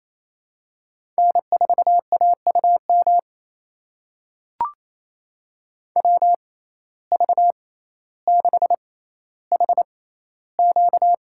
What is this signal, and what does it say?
Morse code, heard as a keyed tone, about 28 wpm, spaced as for 14 wpm, 700 Hz. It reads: N4AUM WV6HQ
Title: USA Call Signs - 5 Characters - 6x Character Spacing - 28wpm